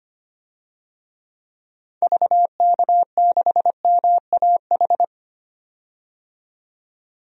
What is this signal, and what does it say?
Morse code, heard as a keyed tone, about 25 wpm, 700 Hz. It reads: VK6MAH
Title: DXCC-100 Call Signs - 6 Characters - 25wpm